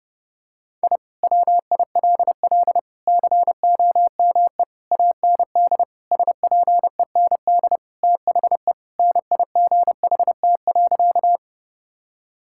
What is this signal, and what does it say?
Morse code, heard as a keyed tone, about 28 wpm, 700 Hz. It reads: I WILL COME AND SPEND THE NIGHT.
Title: Sentences from Top 1000 Words - Review Part 5 - 30wpm